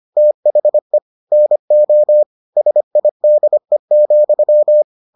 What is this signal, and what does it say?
Morse code, heard as a keyed tone, about 25 wpm, 600 Hz. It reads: THE NO SIDE,